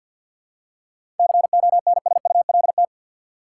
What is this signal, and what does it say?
Morse code, heard as a keyed tone, about 51 wpm, 700 Hz. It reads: CONSULT
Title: Seven Letter Words - 50wpm